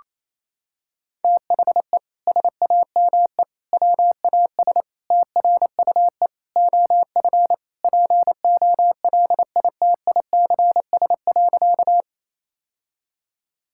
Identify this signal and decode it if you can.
Morse code, heard as a keyed tone, about 28 wpm, 700 Hz. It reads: THE SAME WAS TRUE OF POLITICS.
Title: Sentences from Top 1000 Words - Review Part 6 - 28wpm